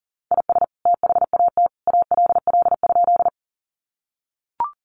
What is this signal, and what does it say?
Morse code, heard as a keyed tone, about 40 wpm, 700 Hz. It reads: IS THAT ALL?
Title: Sentences from Top 100 Words 40wpm